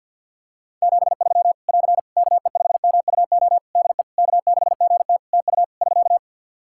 Morse code, heard as a keyed tone, about 48 wpm, 700 Hz. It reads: 73 <AR> KE5MUO DE K6GT TU <SK>
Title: Practice QSO - Exchange 4 - Ending - 50wpm